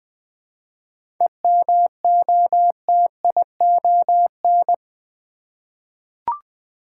Morse code, heard as a keyed tone, about 20 wpm, 700 Hz. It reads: EMOTION
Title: Seven Letter Words - 20wpm